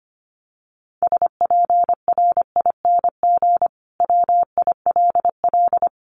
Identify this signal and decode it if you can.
Morse code, heard as a keyed tone, about 25 wpm, 700 Hz. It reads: SPRING WILL
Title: Sentences from Top 1200 Words 25wpm